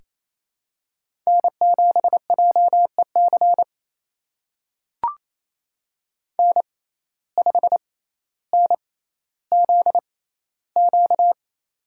Morse code, heard as a keyed tone, about 28 wpm, spaced as for 14 wpm, 700 Hz. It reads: N7JEC N5NZQ
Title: USA Call Signs - 5 Characters - 6x Character Spacing - 28wpm